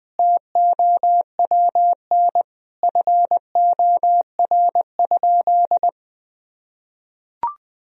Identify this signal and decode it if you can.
Morse code, heard as a keyed tone, about 20 wpm, 700 Hz. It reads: TOWN FOR?
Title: Sentences from Top 1000 Words - Review Part 2 - 20wpm